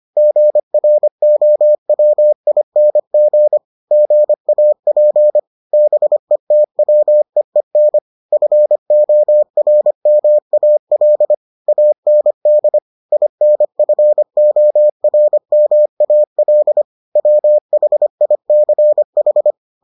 Morse code, heard as a keyed tone, about 25 wpm, 600 Hz. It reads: GROWING GAP BETWEEN FORMAL AND INFORMAL WHICH